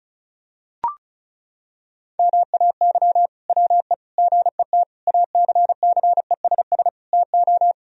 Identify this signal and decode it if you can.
Morse code, heard as a keyed tone, about 34 wpm, 700 Hz. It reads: MAY WE GET ACCESS TO
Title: Sentences from Top 1000 Words - Review Part 4 - 35wpm